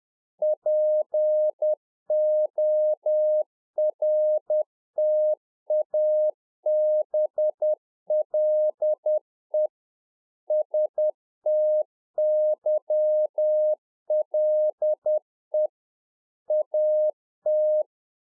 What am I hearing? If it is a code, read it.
Morse code, heard as a keyed tone, about 10 wpm, 610 Hz. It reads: PORTABLE STYLE AT
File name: 2011-04-mp3 / RICK JJ0MPI 10WPM